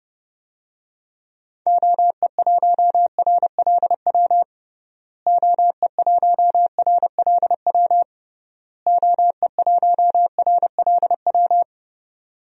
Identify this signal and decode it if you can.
Morse code, heard as a keyed tone, about 30 wpm, 700 Hz. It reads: OE1RLW OE1RLW OE1RLW
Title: DXCC-100 Call Signs - 6 Characters - Repeated 3x - 30wpm